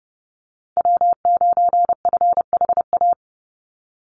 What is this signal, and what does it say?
Morse code, heard as a keyed tone, about 28 wpm, 700 Hz. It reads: W9FHA
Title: NA SOTA Chaser Callsigns 30wpm